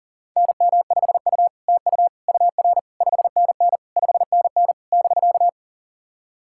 Morse code, heard as a keyed tone, about 39 wpm, 700 Hz. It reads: NM5U TU UR 5NN 5NN <BK>
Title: SOTA QSO Practice 40wpm